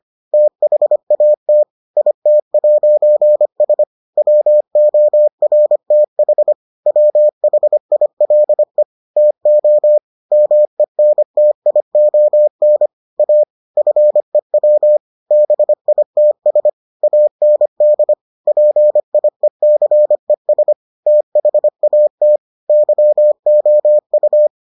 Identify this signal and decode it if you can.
Morse code, heard as a keyed tone, about 25 wpm, 600 Hz. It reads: THAT IT'S WORTH WHILE TO MENTION A FEW BITS AND PIECES THAT YOU